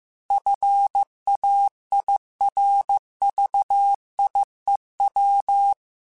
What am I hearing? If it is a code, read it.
Morse code, heard as a keyed tone, about 15 wpm, 800 Hz. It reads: FAIRVIEW